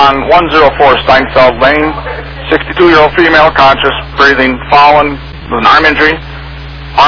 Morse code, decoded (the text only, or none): none